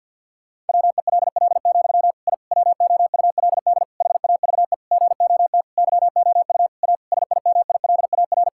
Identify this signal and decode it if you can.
Morse code, heard as a keyed tone, about 52 wpm, 700 Hz. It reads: WELL, I WOULD HAVE GOT YOU A SIMILAR
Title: Sentences from Top 1000 Words - Review Part 5 - 50wpm